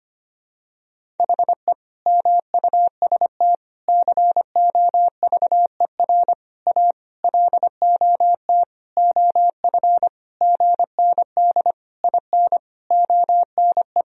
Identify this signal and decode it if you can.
Morse code, heard as a keyed tone, about 25 wpm, 700 Hz. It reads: HE MUST COVER A LOT OF GND IN ONE
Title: Sentences from Top 500 Words and Abbreviations 25wpm